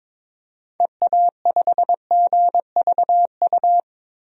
Morse code, heard as a keyed tone, about 22 wpm, 700 Hz. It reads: EA5GVU
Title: DXCC-100 Call Signs - 6 Characters - 22wpm